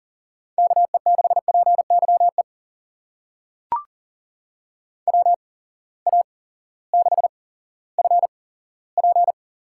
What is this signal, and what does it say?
Morse code, heard as a keyed tone, about 40 wpm, spaced as for 18 wpm, 700 Hz. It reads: KE6PYE WA6FP
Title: USA Call Signs - 6 Characters - 8x Character Spacing - 40wpm